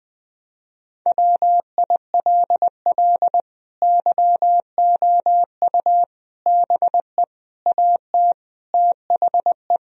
Morse code, heard as a keyed tone, about 20 wpm, 700 Hz. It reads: WILL YOU BE AT THE
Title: Sentences from Top 1000 Words - Review Part 3 - 20wpm